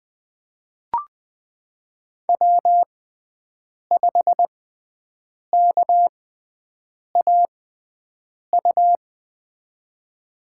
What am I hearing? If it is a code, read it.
Morse code, heard as a keyed tone, about 21 wpm, spaced as for 7 wpm, 700 Hz. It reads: W5KAU
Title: USA Call Signs - 6x Character Spacing - 20wpm